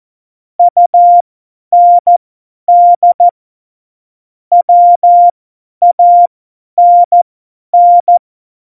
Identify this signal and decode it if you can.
Morse code, heard as a keyed tone, about 14 wpm, spaced as for 10 wpm, 700 Hz. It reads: UND WANN
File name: Morsekurs / Witze / Witz 02. April 2019, 50 F70